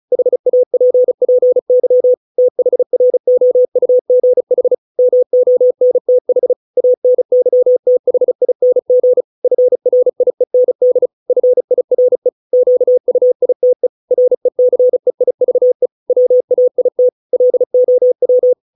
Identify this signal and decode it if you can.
Morse code, heard as a keyed tone, about 35 wpm, 500 Hz. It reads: HAPPY THROUGH MONTH ANYTHING FRIEND FIRE QUITE RECEIVE WAIT LOW